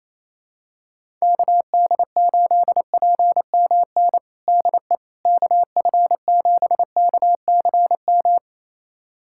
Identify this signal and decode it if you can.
Morse code, heard as a keyed tone, about 27 wpm, 700 Hz. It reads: KD8PMN DE KF7KCM
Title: USA Callsign DE Callsign - 6 Characters - 28wpm